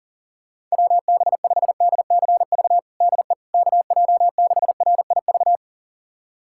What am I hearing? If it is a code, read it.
Morse code, heard as a keyed tone, about 40 wpm, 700 Hz. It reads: WB5DCV DE KJ6RIV
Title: USA Callsign DE Callsign - 6 Characters - 40wpm